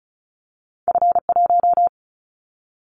Morse code, heard as a keyed tone, about 35 wpm, 700 Hz. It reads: F1
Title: Callsign Prefixes for Top 25 DXCC - 35wpm